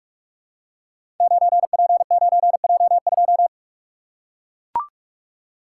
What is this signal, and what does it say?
Morse code, heard as a keyed tone, about 44 wpm, 700 Hz. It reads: 9P9J2
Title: Code Groups - 5 Letter:Numbers - 45wpm